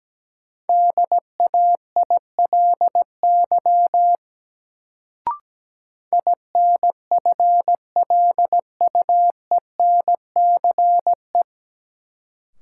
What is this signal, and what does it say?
Morse code, heard as a keyed tone, about 17 wpm, 700 Hz. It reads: DAILY INFLUENCE
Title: Sentences from Top 1100 Words - New Word Drill Alt - 17wpm